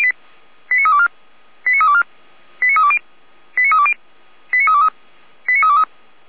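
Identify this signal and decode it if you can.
Five-tone selective call (ZVEI), heard as quick runs of five tones, tones of about 70 ms. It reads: 89324 89324 89320 89320 89323 89323
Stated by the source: Five-tone ZVEI selective call received by SDR, from the audio archive of Feuerwehr Orlamünde.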